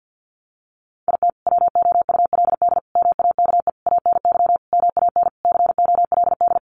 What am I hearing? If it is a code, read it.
Morse code, heard as a keyed tone, about 50 wpm, 700 Hz. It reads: IT WOULD MAKE ANY MAN COLD
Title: Sentences from Top 1100 Words 50wpm